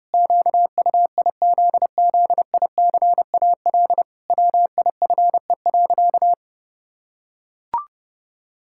Morse code, heard as a keyed tone, about 30 wpm, 700 Hz. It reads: QUIZZICAL WIFE.
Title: Pangram Challenge v2 - 30wpm